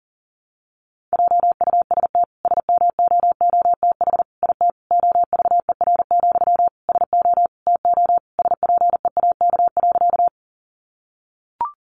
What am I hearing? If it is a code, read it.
Morse code, heard as a keyed tone, about 40 wpm, 700 Hz. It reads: JUST SMOOTH IT OVER, SO TO SPEAK.
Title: Sentences from Top 1400 Words 40wpm